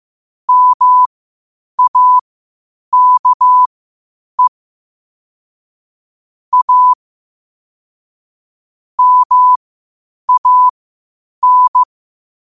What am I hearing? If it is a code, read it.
Morse code, heard as a keyed tone, about 15 wpm, spaced as for 8 wpm, 1000 Hz. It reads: MAKE A MAN
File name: qotdfiles / 8wpm